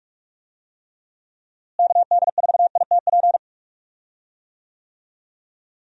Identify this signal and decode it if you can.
Morse code, heard as a keyed tone, about 46 wpm, 700 Hz. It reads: KD4ITP
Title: USA Call Signs - 6 Characters - Repeated 2x - 45wpm